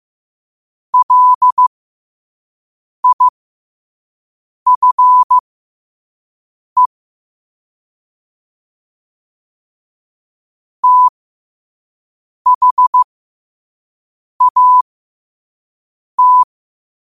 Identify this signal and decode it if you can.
Morse code, heard as a keyed tone, about 15 wpm, spaced as for 5 wpm, 1000 Hz. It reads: LIFE THAT
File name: qotdfiles / 5wpm